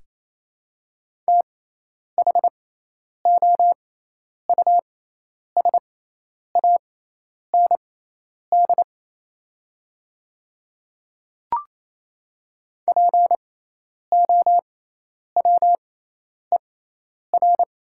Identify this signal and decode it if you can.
Morse code, heard as a keyed tone, about 28 wpm, spaced as for 10 wpm, 700 Hz. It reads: THOUSAND POWER
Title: 6x Character Spacing - Top 500 Words - 28wpm